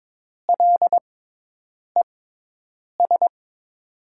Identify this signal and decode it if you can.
Morse code, heard as a keyed tone, about 22 wpm, spaced as for 8 wpm, 700 Hz. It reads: LES
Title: USA Call Signs - 5 Characters - 6x Character Spacing - 22wpm